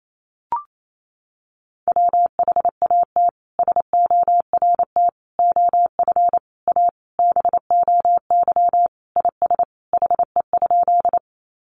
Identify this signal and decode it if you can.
Morse code, heard as a keyed tone, about 28 wpm, 700 Hz. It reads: WHAT SORT OF A BOY IS HE?
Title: Sentences from Top 1300 Words 28wpm